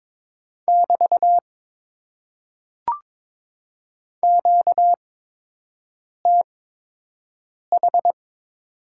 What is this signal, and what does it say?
Morse code, heard as a keyed tone, about 22 wpm, spaced as for 6 wpm, 700 Hz. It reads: <BT> QTH
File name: Top 90 QSO Elements - 8x Character Spacing - 22wpm